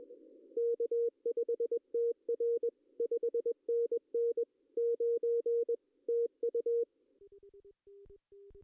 Morse code, heard as a keyed tone, about 21 wpm, 470 Hz. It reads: K5TR 5NN 9 TU
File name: CW / cqww-cw-k5tr